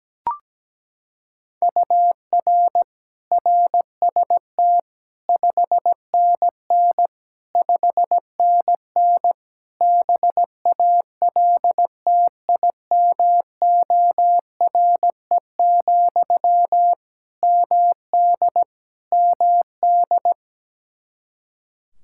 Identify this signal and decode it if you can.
Morse code, heard as a keyed tone, about 17 wpm, 700 Hz. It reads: UR RST 5NN 5NN BALTIMORE, MD MD
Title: Practice QSO - Exchange 1 - Beginning - 17wpm